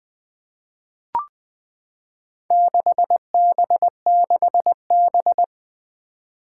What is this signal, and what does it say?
Morse code, heard as a keyed tone, about 20 wpm, 700 Hz. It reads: 6B6B
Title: Four Challenging Letter-Number Combinations 20wpm